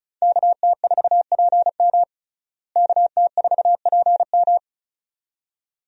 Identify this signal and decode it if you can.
Morse code, heard as a keyed tone, about 34 wpm, 700 Hz. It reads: KT4PM KT4PM
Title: USA Call Signs - 5 Characters - Repeated 2x - 35wpm